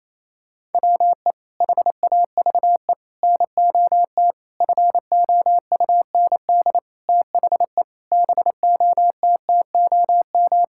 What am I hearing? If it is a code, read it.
Morse code, heard as a keyed tone, about 28 wpm, 700 Hz. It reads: WE HAVE NOT FOUND THE BOTTOM
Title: Sentences from Top 1000 Words - Review Part 6 - 28wpm